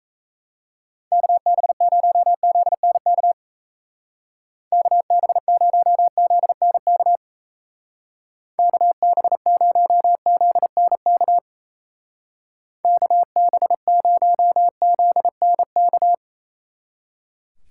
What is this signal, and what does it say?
Morse code, heard as a keyed tone, about 34 wpm, 700 Hz. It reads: KB0ZNK KB0ZNK KB0ZNK KB0ZNK
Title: Speed Racing - USA 2x3 Call Signs 28wpm